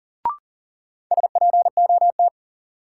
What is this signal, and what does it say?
Morse code, heard as a keyed tone, about 40 wpm, 700 Hz. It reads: SPOT
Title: Sentences from Top 1100 Words - New Word Drill Alt - 40wpm